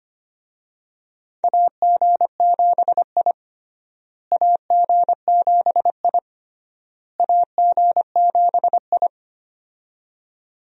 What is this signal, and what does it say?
Morse code, heard as a keyed tone, about 25 wpm, 700 Hz. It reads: AG7I AG7I AG7I
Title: USA Call Signs - 4 Characters - Repeated 3x - 25wpm